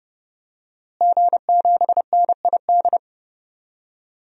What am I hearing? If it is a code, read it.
Morse code, heard as a keyed tone, about 30 wpm, 700 Hz. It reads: G7NID
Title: DXCC-100 Call Signs - 5 Characters - Repeated 3x - 30wpm